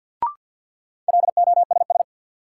Morse code, heard as a keyed tone, about 47 wpm, 700 Hz. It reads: LOSS
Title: Sentences from Top 1400 Words - New Word Drill - 50wpm